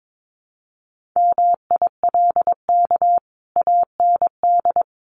Morse code, heard as a keyed tone, about 22 wpm, 700 Hz. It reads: MILK AND
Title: Sentences from Top 1500 Words 22wpm